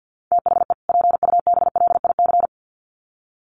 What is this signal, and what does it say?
Morse code, heard as a keyed tone, about 50 wpm, 700 Hz. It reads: THE PUBLIC
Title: Sets of 2 Words 50wpm